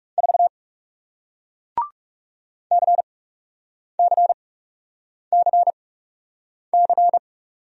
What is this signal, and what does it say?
Morse code, heard as a keyed tone, about 39 wpm, spaced as for 8 wpm, 700 Hz. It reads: 4 CCCC